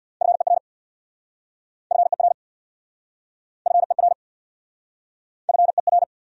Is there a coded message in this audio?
Yes: Morse code from 0.2 to 6.0 s, heard as a keyed tone, about 69 wpm, 700 Hz. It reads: FER FER FER FER